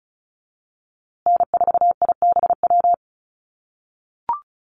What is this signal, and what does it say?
Morse code, heard as a keyed tone, about 34 wpm, 700 Hz. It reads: N4IBW
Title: USA Call Signs Repeated 3x 35wpm